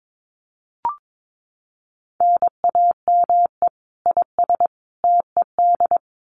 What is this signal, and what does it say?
Morse code, heard as a keyed tone, about 22 wpm, 700 Hz. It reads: NAME IS TED
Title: Common QSO Phrases 22wpm